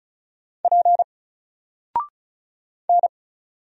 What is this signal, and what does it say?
Morse code, heard as a keyed tone, about 36 wpm, 700 Hz. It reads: P N
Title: Single Letter-Number Alt 35wpm